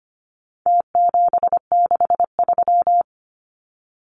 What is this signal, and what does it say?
Morse code, heard as a keyed tone, about 25 wpm, 700 Hz. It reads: T763
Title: Numbers - Contest Style - 4-Digit 25wpm